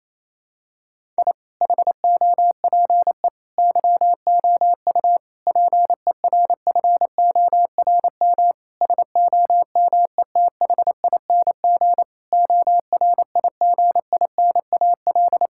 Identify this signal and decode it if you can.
Morse code, heard as a keyed tone, about 28 wpm, 700 Hz. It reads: I HOPE YOU PERFORM SOMETHING ORIGINAL
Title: Sentences from Top 1000 Words - Review Part 6 - 28wpm